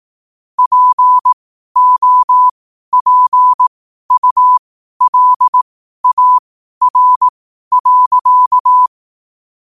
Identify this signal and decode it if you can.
Morse code, heard as a keyed tone, about 18 wpm, spaced as for 12 wpm, 1000 Hz. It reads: POPULAR.
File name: qotdfiles / 12wpm